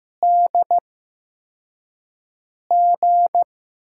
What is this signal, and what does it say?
Morse code, heard as a keyed tone, about 15 wpm, spaced as for 4 wpm, 700 Hz. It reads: DG
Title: USA Call Signs - 8x Character Spacing - 15wpm